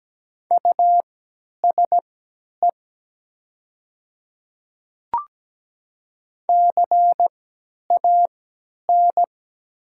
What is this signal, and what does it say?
Morse code, heard as a keyed tone, about 17 wpm, spaced as for 10 wpm, 700 Hz. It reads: USE CAN
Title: 3x Character Spacing - Top 200 Words - 17wpm